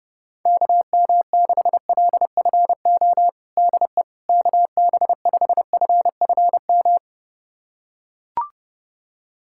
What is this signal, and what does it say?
Morse code, heard as a keyed tone, about 30 wpm, 700 Hz. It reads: KM6LFO DE KB5FFM
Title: USA Callsign DE Callsign - 6 Characters - 30wpm